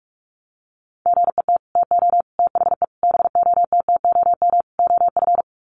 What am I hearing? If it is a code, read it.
Morse code, heard as a keyed tone, about 44 wpm, 700 Hz. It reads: GET TO THE BOTTOM OF
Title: Sets of 5 Words Encore 45wpm